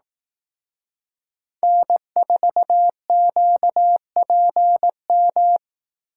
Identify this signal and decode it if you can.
Morse code, heard as a keyed tone, about 18 wpm, 700 Hz. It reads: N4QPM
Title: Speed Racing - USA 1x3 Call Signs 15wpm